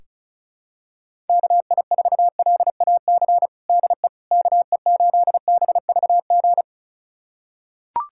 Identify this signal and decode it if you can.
Morse code, heard as a keyed tone, about 36 wpm, 700 Hz. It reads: KI4LAC DE KE8BVG
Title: USA Callsign DE Callsign - 6 Characters - 35wpm